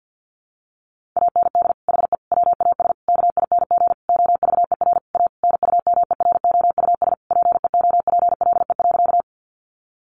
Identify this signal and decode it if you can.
Morse code, heard as a keyed tone, about 50 wpm, 700 Hz. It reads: AND HE WAS KING OVER A NUMEROUS PEOPLE.
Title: Sentences from Top 1400 Words 50wpm